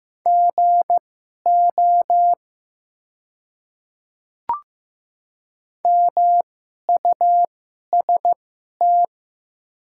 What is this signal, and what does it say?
Morse code, heard as a keyed tone, about 15 wpm, spaced as for 11 wpm, 690 Hz. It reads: GO MUST
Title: 2x Character Spacing - Top 200 Words - 15wpm